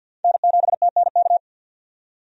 Morse code, heard as a keyed tone, about 45 wpm, 700 Hz. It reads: N7TNK
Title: USA Call Signs - 5 Characters - 50wpm